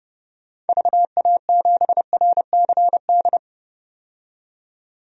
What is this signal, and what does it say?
Morse code, heard as a keyed tone, about 31 wpm, 700 Hz. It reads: VA7RCD
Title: DXCC-100 Call Signs - 6 Characters - 30wpm